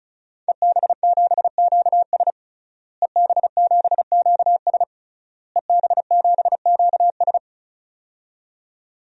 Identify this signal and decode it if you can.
Morse code, heard as a keyed tone, about 36 wpm, 700 Hz. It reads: EB7QS EB7QS EB7QS
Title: DXCC-100 Call Signs - 5 Characters - Repeated 3x - 35wpm